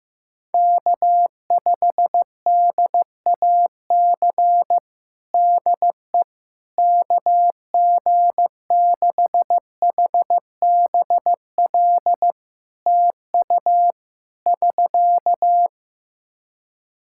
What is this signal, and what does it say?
Morse code, heard as a keyed tone, about 15 wpm, 700 Hz. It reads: K5DAC DE KG6HBL TU <SK>
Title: K1USN SST QSO Practice 15wpm